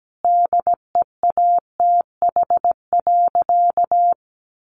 Morse code, heard as a keyed tone, about 17 wpm, 700 Hz. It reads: DEATH.
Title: Sentences from Top 1500 Words 17wpm